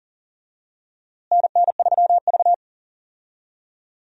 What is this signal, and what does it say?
Morse code, heard as a keyed tone, about 40 wpm, 700 Hz. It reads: NN3V
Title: USA Call Signs - 4 Characters - Repeated 2x - 40wpm